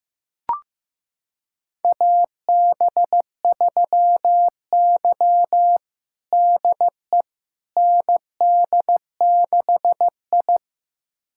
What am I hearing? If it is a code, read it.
Morse code, heard as a keyed tone, about 15 wpm, 700 Hz. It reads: AB3Y DE ND6I
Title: USA Callsign DE Callsign - 4 Characters - 15wpm